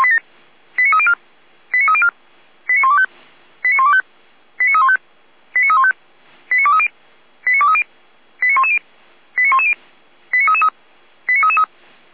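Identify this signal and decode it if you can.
Five-tone selective call (ZVEI), heard as quick runs of five tones, tones of about 70 ms. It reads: 89304 89304 89125 89125 89315 89315 89230 89230 89110 89110 89333
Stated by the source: ZVEI five-tone selective call 89310 received by SDR.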